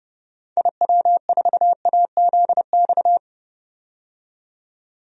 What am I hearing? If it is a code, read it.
Morse code, heard as a keyed tone, about 30 wpm, 700 Hz. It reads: IW4AZX
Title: DXCC-100 Call Signs - 6 Characters - 30wpm